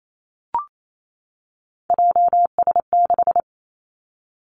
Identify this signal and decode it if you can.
Morse code, heard as a keyed tone, about 28 wpm, 700 Hz. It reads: JS6
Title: Three Challenging Letter-Number Combinations 28wpm